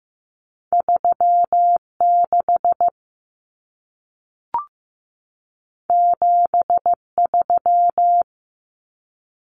Morse code, heard as a keyed tone, about 15 wpm, 700 Hz. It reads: 36 73
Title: Common Cut Numbers - 2-Digit 15wpm